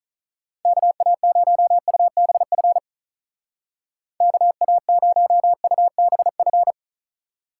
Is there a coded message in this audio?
Yes: Morse code from 0.7 to 6.7 s, heard as a keyed tone, about 44 wpm, 700 Hz. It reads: KA0UBF KA0UBF